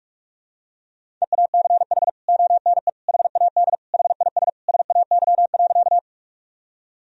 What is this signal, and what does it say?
Morse code, heard as a keyed tone, about 46 wpm, 700 Hz. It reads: EACH ONE HAD HIS SAY.
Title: Sentences from Top 1000 Words - Review Part 1 - 45wpm